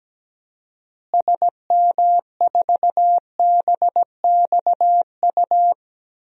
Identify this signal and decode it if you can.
Morse code, heard as a keyed tone, about 17 wpm, 700 Hz. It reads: SM4BXU
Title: DXCC-100 Call Signs - 6 Characters - Repeated 3x - 17wpm